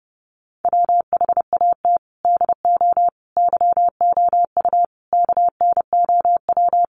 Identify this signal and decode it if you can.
Morse code, heard as a keyed tone, about 28 wpm, 700 Hz. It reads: WHAT DO YOU KNOW
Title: Sentences from Top 100 Words 30wpm